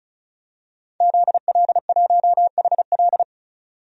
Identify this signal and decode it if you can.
Morse code, heard as a keyed tone, about 36 wpm, 700 Hz. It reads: ZL1HL
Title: DXCC-100 Call Signs - 5 Characters - 35wpm